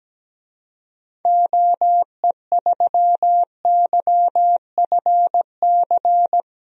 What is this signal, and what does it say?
Morse code, heard as a keyed tone, about 17 wpm, 700 Hz. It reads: OE3YFC
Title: DXCC-100 Call Signs - 6 Characters - Repeated 2x - 17wpm